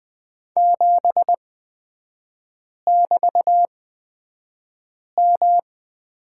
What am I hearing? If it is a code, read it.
Morse code, heard as a keyed tone, about 20 wpm, spaced as for 5 wpm, 700 Hz. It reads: 7<BT>M
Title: 224 Lesson 36e - BT - Three Characters 20wpm